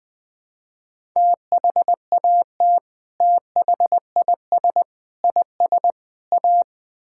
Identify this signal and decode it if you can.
Morse code, heard as a keyed tone, about 20 wpm, 700 Hz. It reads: THAT THIS IS A
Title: Build A Phrase - 4-Word Phrases - 20wpm